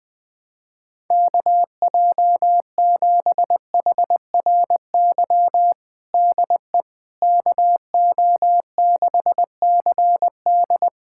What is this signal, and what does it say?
Morse code, heard as a keyed tone, about 20 wpm, 700 Hz. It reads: KJ7HRY DE KO6CD
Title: USA Callsign DE Callsign 20wpm